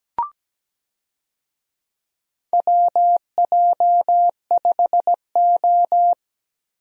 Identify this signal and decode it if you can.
Morse code, heard as a keyed tone, about 17 wpm, 700 Hz. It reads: WJ5O
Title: USA Call Signs - 4 Characters - Repeated 3x - 17wpm